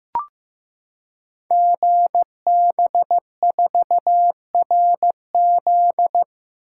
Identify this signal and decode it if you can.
Morse code, heard as a keyed tone, about 15 wpm, 700 Hz. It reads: GB4RZ
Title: Code Groups - 5 Letter:Numbers - 15wpm